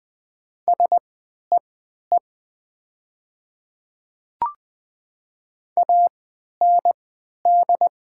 Morse code, heard as a keyed tone, about 20 wpm, spaced as for 11 wpm, 690 Hz. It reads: SEE AND
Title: 3x Character Spacing - Top 200 Words - 20wpm